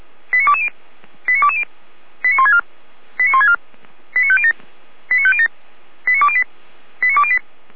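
Five-tone selective call (ZVEI), heard as quick runs of five tones, tones of about 70 ms. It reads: 89220 89220 89164 89164 89507 89507 89208 89208
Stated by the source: ZVEI five-tone selective call received with an SDR.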